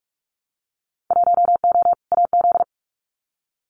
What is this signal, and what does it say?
Morse code, heard as a keyed tone, about 46 wpm, 700 Hz. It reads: 1O AZ
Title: Winter Field Day Exchanges 45wpm